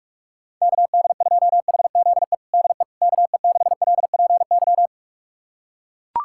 Morse code, heard as a keyed tone, about 41 wpm, 700 Hz. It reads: KD2HZE DE KE6LPY
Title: USA Callsign DE Callsign - 6 Characters - 45wpm